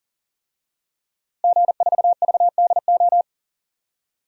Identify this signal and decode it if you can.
Morse code, heard as a keyed tone, about 40 wpm, 700 Hz. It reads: G4VDO
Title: DXCC-100 Call Signs - 5 Characters - Repeated 3x - 40wpm